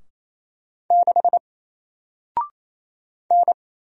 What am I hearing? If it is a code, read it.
Morse code, heard as a keyed tone, about 28 wpm, 700 Hz. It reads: 6 N
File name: Single-Letter-Number-28wpm